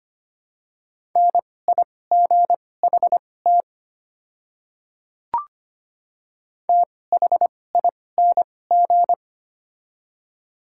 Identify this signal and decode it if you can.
Morse code, heard as a keyed tone, about 25 wpm, spaced as for 18 wpm, 700 Hz. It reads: NIGHT THING
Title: 2x Character Spacing - Top 300 Words - 25wpm